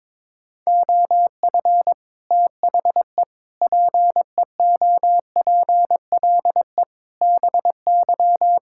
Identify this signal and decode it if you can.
Morse code, heard as a keyed tone, about 22 wpm, 700 Hz. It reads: OF THE PEOPLE BY
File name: practice / 22wpm / Sets of 8 Words - 22wpm